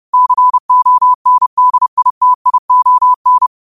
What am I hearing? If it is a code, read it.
Morse code, heard as a keyed tone, about 30 wpm, 1000 Hz. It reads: CONDITION
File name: qotdfiles / 30wpm